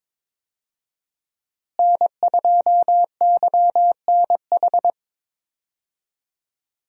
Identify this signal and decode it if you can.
Morse code, heard as a keyed tone, about 22 wpm, 700 Hz. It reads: N2YNH
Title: USA Call Signs - 5 Characters - Repeated 3x - 22wpm